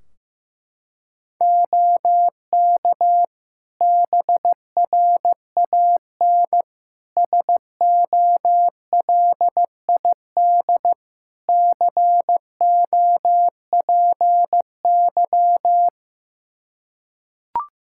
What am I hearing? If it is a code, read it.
Morse code, heard as a keyed tone, about 15 wpm, 700 Hz. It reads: OK BRAN SOLID COPY
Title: Practice QSO - Exchange 3 - More Info - 15wpm